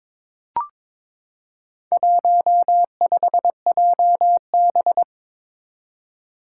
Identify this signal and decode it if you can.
Morse code, heard as a keyed tone, about 23 wpm, 700 Hz. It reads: 15JB
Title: Four Challenging Letter-Number Combinations 22wpm